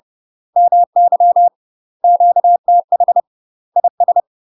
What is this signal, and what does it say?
Morse code, heard as a keyed tone, about 30 wpm, 700 Hz. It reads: MY QTH IS
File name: UK MASTER QSOs TRAINING / 30WPM / UK MASTER QSOs 0045